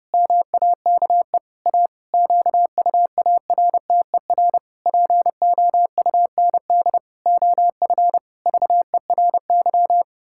Morse code, heard as a keyed tone, about 30 wpm, 700 Hz. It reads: MAKE A QUARTER POUND OF VERY
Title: Pangram Challenge v2 - 30wpm